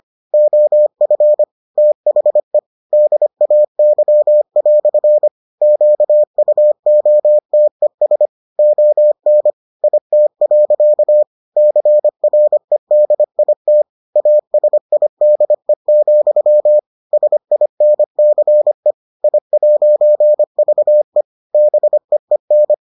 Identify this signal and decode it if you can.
Morse code, heard as a keyed tone, about 25 wpm, 600 Hz. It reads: OF THE DAY" QUOTES ON IT. CREDIT ASIDE, SINCE I'VE BEEN